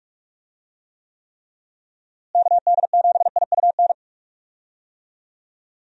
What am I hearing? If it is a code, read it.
Morse code, heard as a keyed tone, about 41 wpm, 700 Hz. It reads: KD7IUN
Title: USA Call Signs - 6 Characters - Repeated 3x - 45wpm